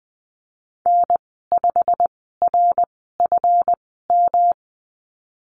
Morse code, heard as a keyed tone, about 20 wpm, spaced as for 15 wpm, 700 Hz. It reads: N5RFM
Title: 193 Lesson 31g - Letter v - Call Signs 20wpm